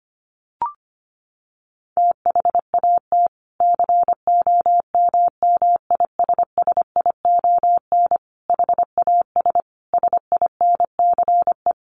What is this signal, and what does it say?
Morse code, heard as a keyed tone, about 25 wpm, 700 Hz. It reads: THAT COMMISSION HAS SINCE
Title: Sentences from Top 1400 Words 25wpm